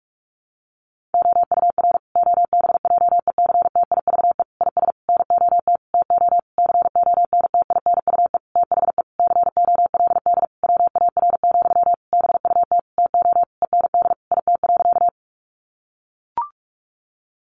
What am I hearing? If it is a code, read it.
Morse code, heard as a keyed tone, about 44 wpm, 700 Hz. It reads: OUR OBJECTIVE IS NOT TO CONTINUE THE COLD WAR, BUT TO END IT.
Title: Sentences from Top 1200 Words 45wpm